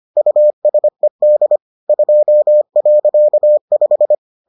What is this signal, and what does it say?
Morse code, heard as a keyed tone, about 25 wpm, 600 Hz. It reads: USED 2.5